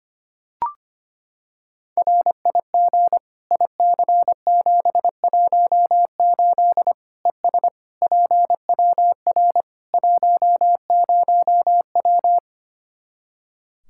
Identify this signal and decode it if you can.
Morse code, heard as a keyed tone, about 25 wpm, 700 Hz. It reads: RIG IC718 ES PWR 10W
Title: Practice QSO - Exchange 2 - Info - 25wpm